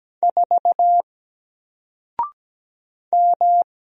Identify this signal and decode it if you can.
Morse code, heard as a keyed tone, about 17 wpm, 700 Hz. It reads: 4 M
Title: Single Character (Letter:Number:Punctuation:Prosign) 17wpm